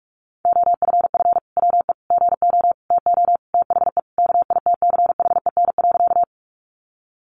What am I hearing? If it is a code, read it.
Morse code, heard as a keyed tone, about 44 wpm, 700 Hz. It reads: OFF WE GO TO THE KITCHEN.
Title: Sentences from Top 1400 Words 45wpm